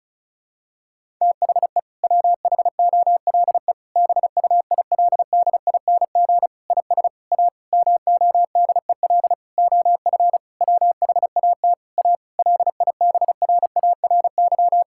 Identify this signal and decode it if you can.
Morse code, heard as a keyed tone, about 34 wpm, 700 Hz. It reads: THE WHOLE BUILDING IS A MODEL OF WHAT A LIBRARY
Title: Sentences from Top 1000 Words - Review Part 5 - 35wpm